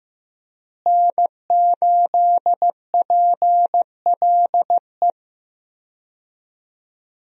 Morse code, heard as a keyed tone, about 15 wpm, 700 Hz. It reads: N8PLE
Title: USA Call Signs - 5 Characters - Repeated 3x - 15wpm